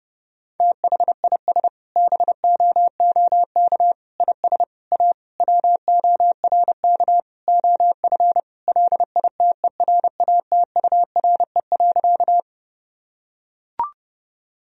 Morse code, heard as a keyed tone, about 30 wpm, 700 Hz. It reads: THIS BOOK IS A WORK OF LITERATURE.
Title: Sentences from Top 1000 Words - Review Part 3 - 30wpm